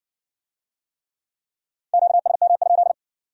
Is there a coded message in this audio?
Yes: Morse code from 1.9 to 2.9 s, heard as a keyed tone, about 54 wpm, 700 Hz. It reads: QSM?